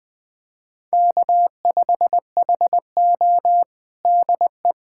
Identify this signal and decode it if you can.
Morse code, heard as a keyed tone, about 21 wpm, 700 Hz. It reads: K5HO DE
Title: USA Callsign DE Callsign - 4 Characters - 20wpm